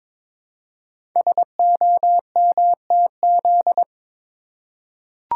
Morse code, heard as a keyed tone, about 22 wpm, 700 Hz. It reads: SOMTZ
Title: Code Groups - 5 Letter:Numbers - 22wpm